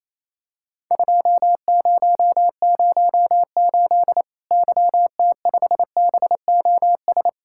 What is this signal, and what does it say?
Morse code, heard as a keyed tone, about 28 wpm, 700 Hz. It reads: 2008 YT5BOS
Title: Worked All Europe DX Contest - QTC Exchange - 28wpm